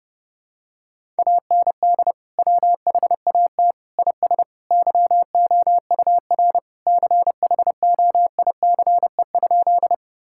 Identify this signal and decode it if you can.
Morse code, heard as a keyed tone, about 28 wpm, 700 Hz. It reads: AND WHAT IS YOUR CHOICE?
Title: Sentences from Top 1000 Words - Review Part 6 - 30wpm